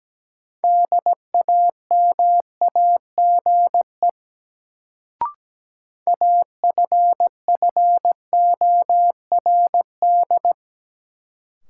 Morse code, heard as a keyed tone, about 17 wpm, 700 Hz. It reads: DAMAGE AFFORD
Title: Sentences from Top 1100 Words - New Word Drill Alt - 17wpm